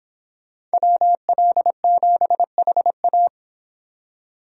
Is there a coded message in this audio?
Yes: Morse code from 0.7 to 3.3 s, heard as a keyed tone, about 25 wpm, 700 Hz. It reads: WL7HA